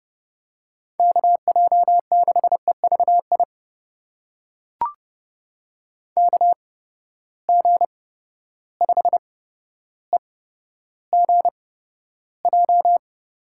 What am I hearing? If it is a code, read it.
Morse code, heard as a keyed tone, about 31 wpm, spaced as for 13 wpm, 700 Hz. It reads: KJ6EVI KG5EGJ
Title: USA Call Signs - 6 Characters - 8x Character Spacing - 30wpm